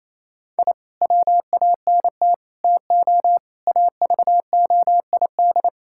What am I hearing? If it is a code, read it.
Morse code, heard as a keyed tone, about 28 wpm, 700 Hz. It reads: I WANT TO AVOID
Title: Sentences from Top 1000 Words - Review Part 6 - 28wpm